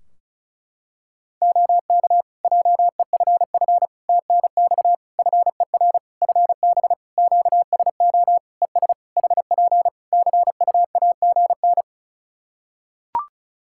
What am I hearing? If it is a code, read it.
Morse code, heard as a keyed tone, about 36 wpm, 700 Hz. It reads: OK JEFF TNX FER FB QSO ES HP CUAGN
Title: Practice QSO - Exchange 4 - Ending - 35wpm